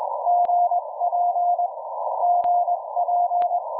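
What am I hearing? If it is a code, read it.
Morse code, heard as a keyed tone, about 21 wpm, 710 Hz. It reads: GP GP
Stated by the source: WAKE Up! QRP Sprint contest QSO with UI7K, who switched between vertical and L-shaped antennas; poor propagation with rapidly changing signal levels.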